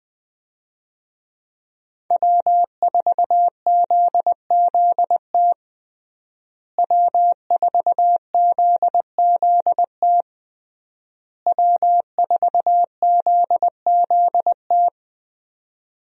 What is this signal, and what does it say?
Morse code, heard as a keyed tone, about 21 wpm, 700 Hz. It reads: W4ZZT W4ZZT W4ZZT
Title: USA Call Signs - 5 Characters - Repeated 3x - 20wpm